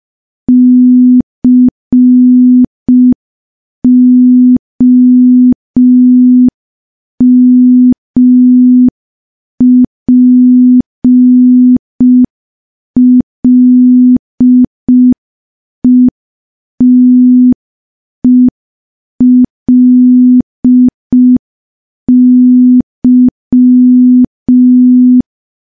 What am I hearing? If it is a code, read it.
Morse code, heard as a keyed tone, about 5 wpm, 250 Hz. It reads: COMPLETELY
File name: morse / morse-training-05WPM